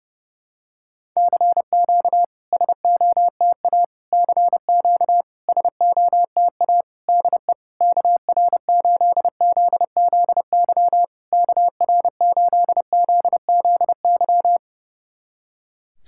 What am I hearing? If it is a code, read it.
Morse code, heard as a keyed tone, about 30 wpm, 700 Hz. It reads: CQ SOTA CQ SOTA DE KR8ZZY KR8ZZY
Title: SOTA QSO Practice 30wpm